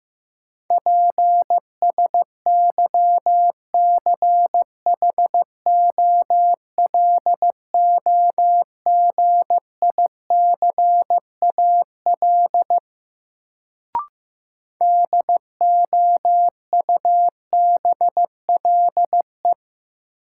Morse code, heard as a keyed tone, about 15 wpm, 700 Hz. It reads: PSYCHOLOGICAL DOUBLE
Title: Sentences from Top 1400 Words - New Word Drill - 15wpm